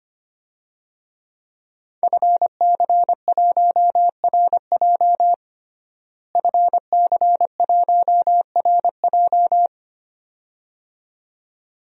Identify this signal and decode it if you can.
Morse code, heard as a keyed tone, about 25 wpm, 700 Hz. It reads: FC1RJ FC1RJ
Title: DXCC-100 Call Signs - 5 Characters - Repeated 2x - 25wpm